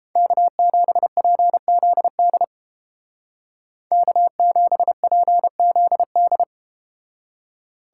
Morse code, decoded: K7PZD K7PZD

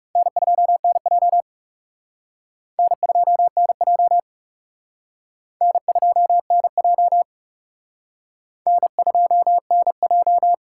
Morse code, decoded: N2NJ N2NJ N2NJ N2NJ

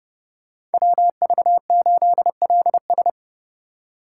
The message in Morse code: WV8LS

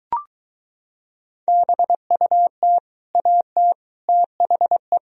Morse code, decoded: BUT AT THE

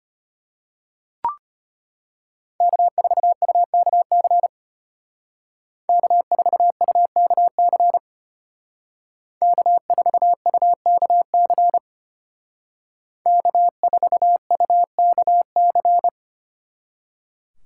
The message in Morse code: K4UKC K4UKC K4UKC K4UKC

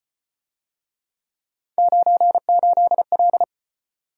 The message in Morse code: 98L

